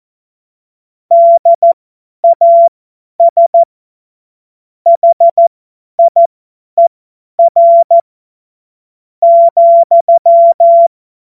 Morse code, heard as a keyed tone, about 14 wpm, spaced as for 9 wpm, 680 Hz. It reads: DAS HIER,
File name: Morsekurs / Witze / Witz 23. August 2022 50 F70